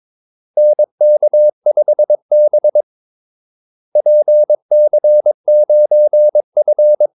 Morse code, heard as a keyed tone, about 22 wpm, 600 Hz. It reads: NK5B PC9F
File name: Callsigns-22wpm